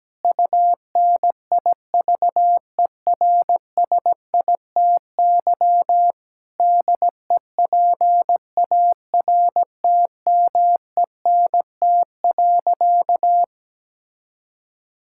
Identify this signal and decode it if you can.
Morse code, heard as a keyed tone, about 17 wpm, 700 Hz. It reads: UNIVERSITY DEPARTMENT.
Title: Sentences from Top 1000 Words - Review Part 5 - 17wpm